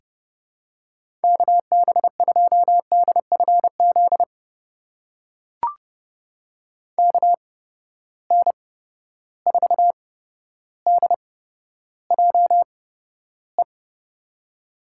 Morse code, heard as a keyed tone, about 32 wpm, spaced as for 13 wpm, 700 Hz. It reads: KB2DFZ KN4DJE
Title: USA Call Signs - 6 Characters - 8x Character Spacing - 30wpm